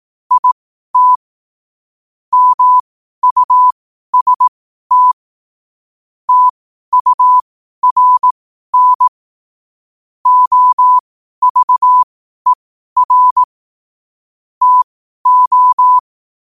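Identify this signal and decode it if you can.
Morse code, heard as a keyed tone, about 18 wpm, spaced as for 12 wpm, 1000 Hz. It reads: IT MUST TURN OVER TO
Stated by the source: Morse practice sent at 12 wpm.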